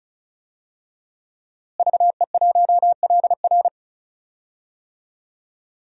Morse code, heard as a keyed tone, about 34 wpm, 700 Hz. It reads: VE1LR
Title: DXCC-100 Call Signs - 5 Characters - Repeated 2x - 35wpm